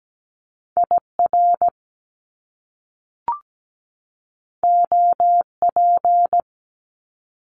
Morse code, heard as a keyed tone, about 17 wpm, 700 Hz. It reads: IR OP
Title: Top Two Letter Combinations in Words 17wpm